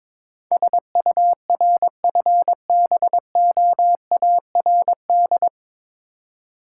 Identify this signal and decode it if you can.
Morse code, heard as a keyed tone, about 22 wpm, 690 Hz. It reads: SURFBOARD
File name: Difficult Words with Hints - 22wpm